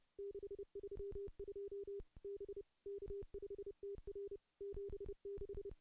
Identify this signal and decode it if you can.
Morse code, heard as a keyed tone, about 30 wpm, 400 Hz. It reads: 632 B K5TR 76
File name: contests / CW-SS-k5tr- -0400